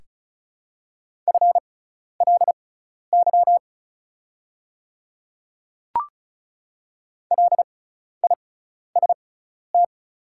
Morse code, heard as a keyed tone, about 36 wpm, spaced as for 12 wpm, 700 Hz. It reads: FLY LIST
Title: 6x Character Spacing - Top 500 Words - 35wpm